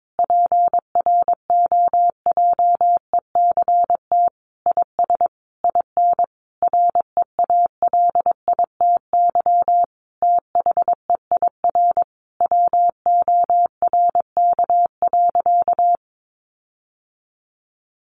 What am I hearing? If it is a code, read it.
Morse code, heard as a keyed tone, about 22 wpm, 700 Hz. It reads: PROJECT IS IN REALITY THEIR WORK.
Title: Sentences from Top 1400 Words 22wpm